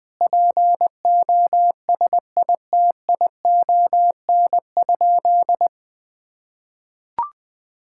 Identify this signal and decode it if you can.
Morse code, heard as a keyed tone, about 20 wpm, 700 Hz. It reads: POSITION?
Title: Sentences from Top 1000 Words - Review Part 5 - 20wpm